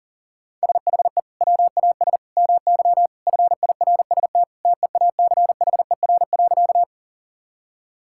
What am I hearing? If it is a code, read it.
Morse code, heard as a keyed tone, about 40 wpm, 700 Hz. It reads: SHE WAS MY FIRST TEACHER.
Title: Sentences from Top 1000 Words - Review Part 6 - 40wpm